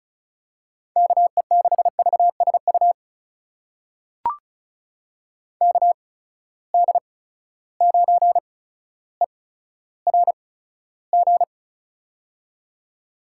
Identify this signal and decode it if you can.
Morse code, heard as a keyed tone, about 34 wpm, spaced as for 15 wpm, 700 Hz. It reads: KE6VSU KD9ERG
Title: USA Call Signs - 6 Characters - 8x Character Spacing - 35wpm